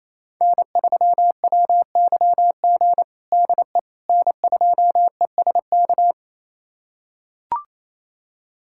Morse code, heard as a keyed tone, about 28 wpm, 700 Hz. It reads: N3WYG DE N2ESK
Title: USA Callsign DE Callsign - 5 Characters - 28wpm